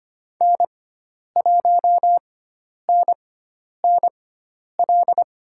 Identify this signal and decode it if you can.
Morse code, heard as a keyed tone, about 25 wpm, spaced as for 10 wpm, 700 Hz. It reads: N1NNL
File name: Lesson 16g - Letter c - Call Signs 25wpm